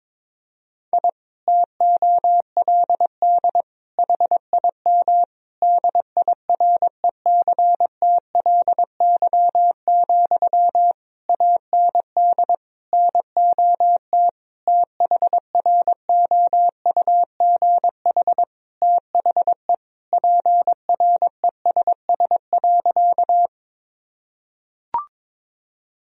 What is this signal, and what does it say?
Morse code, heard as a keyed tone, about 22 wpm, 700 Hz. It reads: I TOLD HIM DIRECTLY, AND NOT THROUGH THE PRESS.
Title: Sentences from Top 1000 Words - Review Part 1 - 22wpm